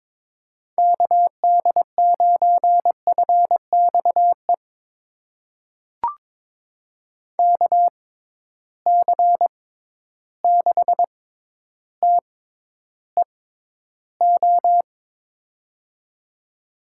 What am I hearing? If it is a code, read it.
Morse code, heard as a keyed tone, about 21 wpm, spaced as for 11 wpm, 700 Hz. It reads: KD9FXE KC6TEO
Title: USA Call Signs - 6 Characters - 6x Character Spacing - 22wpm